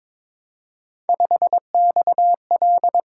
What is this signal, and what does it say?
Morse code, heard as a keyed tone, about 22 wpm, 700 Hz. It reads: 5XL
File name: practice / 22wpm / Three Letter-Number Combinations - 22wpm